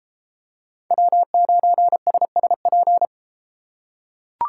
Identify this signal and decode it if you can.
Morse code, heard as a keyed tone, about 30 wpm, 700 Hz. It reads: W9SSP